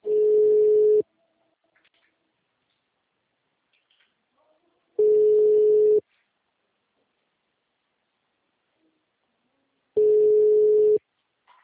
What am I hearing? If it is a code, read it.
Ringback tone (Europe, 425 Hz).
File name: out- -2039- - - .109